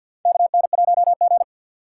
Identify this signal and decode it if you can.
Morse code, heard as a keyed tone, about 49 wpm, 700 Hz. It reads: KN1G
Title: USA Call Signs - 6x Character Spacing - 50wpm